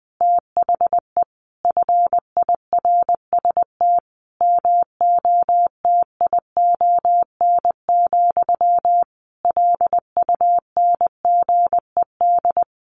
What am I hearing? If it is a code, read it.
Morse code, heard as a keyed tone, about 20 wpm, 700 Hz. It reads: THE FIRST MOTION, LUNGED
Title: A Sound Of Thunder 20wpm